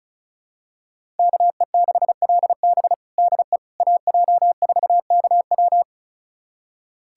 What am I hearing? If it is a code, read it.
Morse code, heard as a keyed tone, about 36 wpm, 700 Hz. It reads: KE6LB DE AJ4KW
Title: USA Callsign DE Callsign - 5 Characters - 35wpm